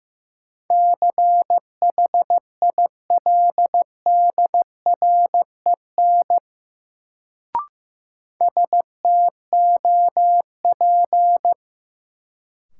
Morse code, heard as a keyed tone, about 15 wpm, 700 Hz. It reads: CHILDREN STOP
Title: Sentences from Top 300 Words - New Word Drill - 15wpm